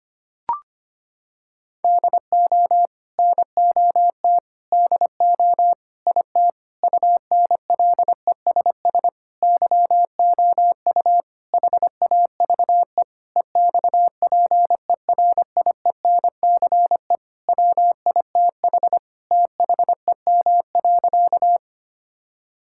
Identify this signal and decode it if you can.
Morse code, heard as a keyed tone, about 25 wpm, 700 Hz. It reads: DO NOT DO IT UNLESS YOU HAVE EXPERIENCE WITH THEM.
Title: Sentences from Top 1000 Words - Review Part 6 - 25wpm